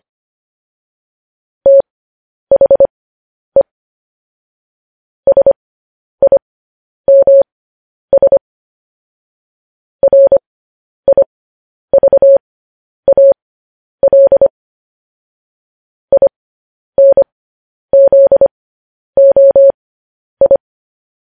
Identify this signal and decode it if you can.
Morse code, heard as a keyed tone, about 25 wpm, spaced as for 10 wpm, 550 Hz. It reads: THE SIMS RIVAL INZOI
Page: news-headlines-in-morse-code-at-10-wpm